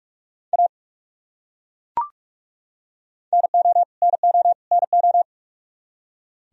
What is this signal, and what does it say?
Morse code, heard as a keyed tone, about 41 wpm, 700 Hz. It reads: A NO NO NO
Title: Top 100 Words Repeated 45wpm